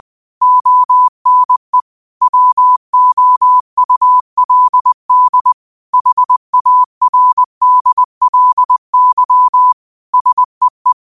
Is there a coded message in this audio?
Yes: Morse code from 0.4 to 10.9 s, heard as a keyed tone, about 20 wpm, 1000 Hz. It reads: ONE WOULD HARDLY SEE